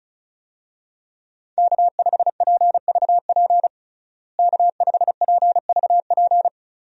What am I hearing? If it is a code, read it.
Morse code, heard as a keyed tone, about 36 wpm, 700 Hz. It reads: K5PVP K5PVP
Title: USA Call Signs - 5 Characters - Repeated 3x - 35wpm